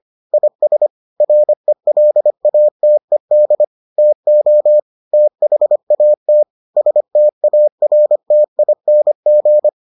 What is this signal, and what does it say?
Morse code, heard as a keyed tone, about 25 wpm, 600 Hz. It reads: IS RELATED TO THAT STARTING